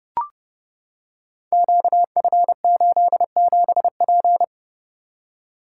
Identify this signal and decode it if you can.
Morse code, heard as a keyed tone, about 29 wpm, 700 Hz. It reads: QF87P
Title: Code Groups - 5 Letter:Numbers - 30wpm